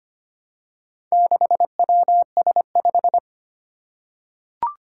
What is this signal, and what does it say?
Morse code, heard as a keyed tone, about 25 wpm, 700 Hz. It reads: 6WS5
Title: Four Challenging Letter-Number Combinations 25wpm